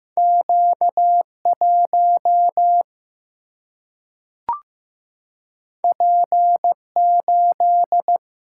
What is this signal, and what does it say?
Morse code, heard as a keyed tone, about 15 wpm, 700 Hz. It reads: Q1 P8
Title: Single-Letter Single-Number 15wpm